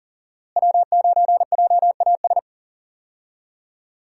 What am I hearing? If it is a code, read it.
Morse code, heard as a keyed tone, about 40 wpm, 700 Hz. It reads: W9JAS
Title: USA Call Signs - 5 Characters - Repeated 3x - 40wpm